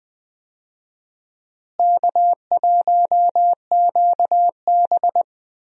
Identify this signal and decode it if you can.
Morse code, heard as a keyed tone, about 20 wpm, 700 Hz. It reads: K1QB
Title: USA Call Signs - 4 Characters - Repeated 2x - 20wpm